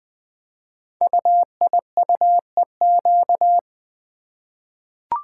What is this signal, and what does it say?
Morse code, heard as a keyed tone, about 20 wpm, 700 Hz. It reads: UIUEQ
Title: Code Groups - 5 Letters - 20wpm